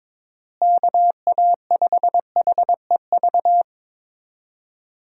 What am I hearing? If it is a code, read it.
Morse code, heard as a keyed tone, about 22 wpm, 700 Hz. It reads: KA5HEV
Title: USA Call Signs - 6 Characters - Repeated 3x - 22wpm